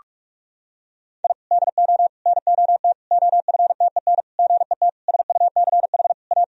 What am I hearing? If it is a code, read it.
Morse code, heard as a keyed tone, about 46 wpm, 700 Hz. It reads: I DO NOT OFTEN GET SUCH A
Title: Sentences from Top 1000 Words - Review Part 1 - 45wpm